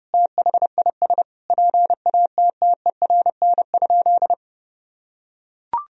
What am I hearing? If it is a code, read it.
Morse code, heard as a keyed tone, about 30 wpm, 700 Hz. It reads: THIS PATTERN?
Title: Sentences from Top 1000 Words - Review Part 2 - 30wpm